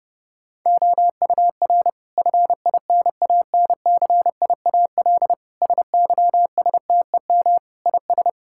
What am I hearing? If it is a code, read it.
Morse code, heard as a keyed tone, about 30 wpm, 700 Hz. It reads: OUR FINANCIAL SYSTEM IS
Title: Sentences from Top 1000 Words - Review Part 5 - 30wpm